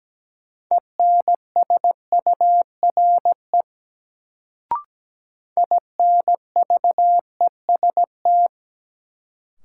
Morse code, heard as a keyed tone, about 17 wpm, 700 Hz. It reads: ENSURE INVEST
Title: Sentences from Top 1100 Words - New Word Drill Alt - 17wpm